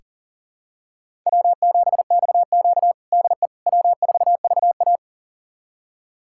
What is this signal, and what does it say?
Morse code, heard as a keyed tone, about 40 wpm, 700 Hz. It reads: W7XQ DE W4VA
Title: USA Callsign DE Callsign - 4 Characters - 40wpm